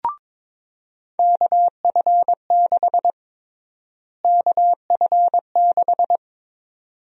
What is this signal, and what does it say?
Morse code, heard as a keyed tone, about 23 wpm, 700 Hz. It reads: KF6 KF6
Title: USA Call Sign Prefixes - Repeated 2x - 22wpm